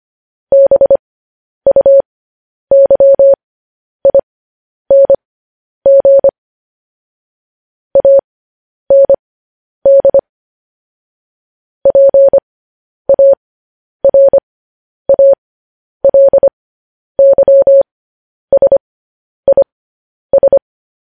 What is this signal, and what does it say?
Morse code, heard as a keyed tone, about 25 wpm, spaced as for 10 wpm, 550 Hz. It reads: BUYING AND PARALYSIS
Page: News Headlines in Morse Code at 10 WPM